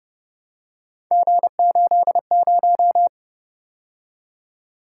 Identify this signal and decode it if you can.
Morse code, heard as a keyed tone, about 32 wpm, 700 Hz. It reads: G80